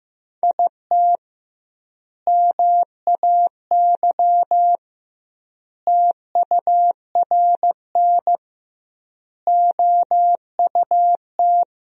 Morse code, heard as a keyed tone, about 16 wpm, 700 Hz. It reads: IT MAY TURN OUT
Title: 2x Word-Spacing - Sentences from Top 400 Words - 15wpm